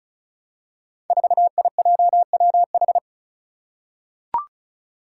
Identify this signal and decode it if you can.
Morse code, heard as a keyed tone, about 36 wpm, 700 Hz. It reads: 4IJWH
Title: Five Challenging Letter-Number Combinations 35wpm